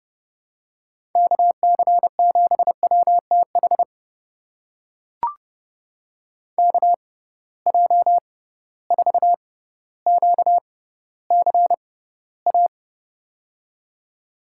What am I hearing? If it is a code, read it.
Morse code, heard as a keyed tone, about 30 wpm, spaced as for 15 wpm, 700 Hz. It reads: KC7WTH KJ4QCA
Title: USA Call Signs - 6 Characters - 6x Character Spacing - 30wpm